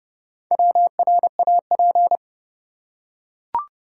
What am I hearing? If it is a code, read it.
Morse code, heard as a keyed tone, about 30 wpm, 700 Hz. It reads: WRAP
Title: Four Letter Words - 30wpm